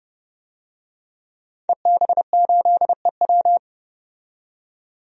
Morse code, heard as a keyed tone, about 30 wpm, 700 Hz. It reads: EB8EW
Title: DXCC-100 Call Signs - 5 Characters - 30wpm